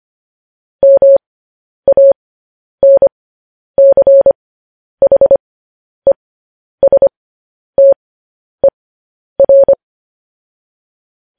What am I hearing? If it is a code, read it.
Morse code, heard as a keyed tone, about 25 wpm, spaced as for 10 wpm, 550 Hz. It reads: MANCHESTER
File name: podcast / News headlines-1x-25x10